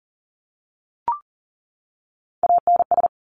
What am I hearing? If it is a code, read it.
Morse code, heard as a keyed tone, about 40 wpm, 700 Hz. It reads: ANS
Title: Top Three Letter Combinations in Words 40wpm